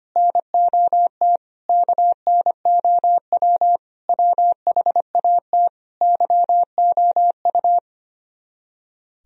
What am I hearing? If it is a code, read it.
Morse code, heard as a keyed tone, about 25 wpm, 700 Hz. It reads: NOT KNOW WHAT YOU
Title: Speed-Racing - Sets of 5 Words 25wpm